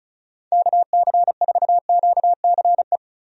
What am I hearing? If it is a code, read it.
Morse code, heard as a keyed tone, about 36 wpm, 700 Hz. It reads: KC4QCE
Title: USA Call Signs - 6 Characters - 35wpm